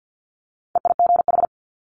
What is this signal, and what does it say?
Morse code, heard as a keyed tone, about 50 wpm, 700 Hz. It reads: EIGH
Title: Multi-letter Phonemes Quick Fire 50wpm